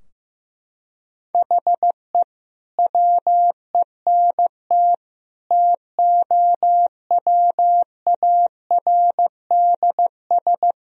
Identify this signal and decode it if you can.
Morse code, heard as a keyed tone, about 15 wpm, 700 Hz. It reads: HE WENT TOWARDS
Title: Sentences from Top 1000 Words - Review Part 2 - 15wpm